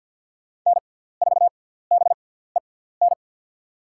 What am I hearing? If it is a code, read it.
Morse code, heard as a keyed tone, about 52 wpm, spaced as for 17 wpm, 690 Hz. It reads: N4BEN
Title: USA Call Signs - 5 Characters - 6x Character Spacing - 50wpm